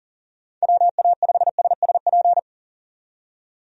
Morse code, heard as a keyed tone, about 37 wpm, 690 Hz. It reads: WA5SSP
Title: USA Call Signs - 6 Characters - 40wpm